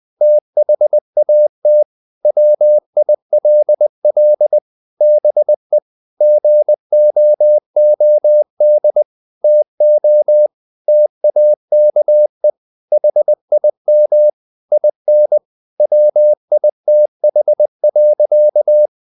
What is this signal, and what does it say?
Morse code, heard as a keyed tone, about 20 wpm, 600 Hz. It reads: THAT WILL BE GOOD TO TAKE HIM IN WITH.
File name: the old man and the sea part 2 20wpm